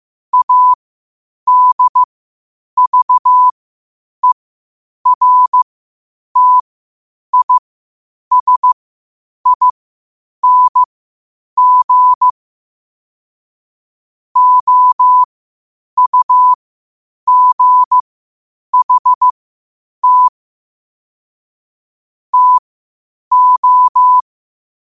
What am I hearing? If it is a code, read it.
Morse code, heard as a keyed tone, about 15 wpm, spaced as for 8 wpm, 1000 Hz. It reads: ADVERTISING OUGHT TO
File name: qotdfiles / 8wpm